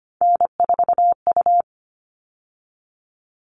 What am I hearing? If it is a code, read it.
Morse code, heard as a keyed tone, about 25 wpm, 700 Hz. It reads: N4U
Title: All Cut Numbers - 3-Digit 25wpm